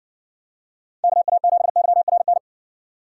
Morse code, heard as a keyed tone, about 60 wpm, 700 Hz. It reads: KA7YRN